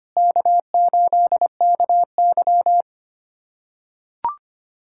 Morse code, heard as a keyed tone, about 25 wpm, 700 Hz. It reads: K8KY
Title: USA Callsign DE Callsign 25wpm